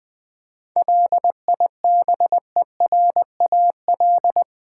Morse code, heard as a keyed tone, about 20 wpm, 700 Hz. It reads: LIBERAL
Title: 7 Letter Words - First 3 Letters Given - 20wpm